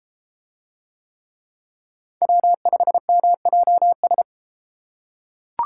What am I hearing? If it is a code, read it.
Morse code, heard as a keyed tone, about 32 wpm, 700 Hz. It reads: W5MJS